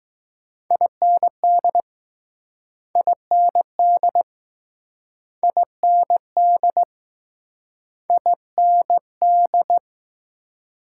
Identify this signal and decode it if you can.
Morse code, heard as a keyed tone, about 20 wpm, 700 Hz. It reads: IND IND IND IND